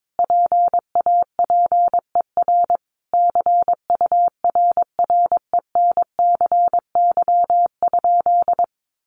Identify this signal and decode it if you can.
Morse code, heard as a keyed tone, about 22 wpm, 700 Hz. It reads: PAPER CURRENCY?
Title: Sentences from Top 1500 Words 22wpm